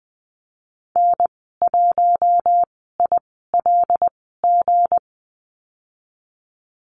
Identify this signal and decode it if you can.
Morse code, heard as a keyed tone, about 20 wpm, spaced as for 14 wpm, 700 Hz. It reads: N1ILG
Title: 167 Lesson 27g - Number 7 - Call Signs 20wpm